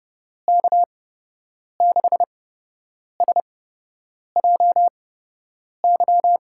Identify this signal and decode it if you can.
Morse code, heard as a keyed tone, about 30 wpm, spaced as for 8 wpm, 700 Hz. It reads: K6SJY
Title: USA Call Signs - 6 Characters - 8x Character Spacing - 30wpm